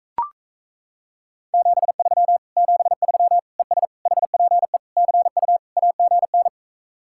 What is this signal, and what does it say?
Morse code, heard as a keyed tone, about 35 wpm, 700 Hz. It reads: 73 73 ES HPE CU AGN